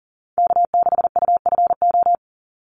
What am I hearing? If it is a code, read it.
Morse code, heard as a keyed tone, about 40 wpm, 700 Hz. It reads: K6UFO
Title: Call Signs - 40wpm